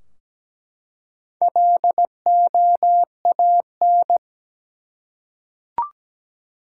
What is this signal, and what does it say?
Morse code, heard as a keyed tone, about 17 wpm, 700 Hz. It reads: LOAN
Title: Four Letter Words - 17wpm